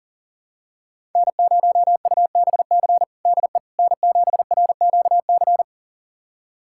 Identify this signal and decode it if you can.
Morse code, heard as a keyed tone, about 40 wpm, 700 Hz. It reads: N0UBC DE N7RQC
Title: USA Callsign DE Callsign - 5 Characters - 40wpm